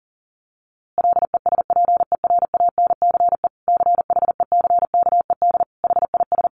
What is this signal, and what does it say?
Morse code, heard as a keyed tone, about 40 wpm, 700 Hz. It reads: LESPERANCE CHECKED HIS
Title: A Sound Of Thunder 40wpm